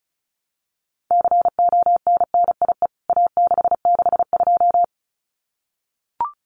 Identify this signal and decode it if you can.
Morse code, heard as a keyed tone, about 34 wpm, 700 Hz. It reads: CONNIE A662
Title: CWT Exchange 35wpm